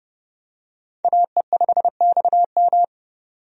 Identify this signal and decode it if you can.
Morse code, heard as a keyed tone, about 30 wpm, 700 Hz. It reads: AE5XM
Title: USA Call Signs - 5 Characters - 30wpm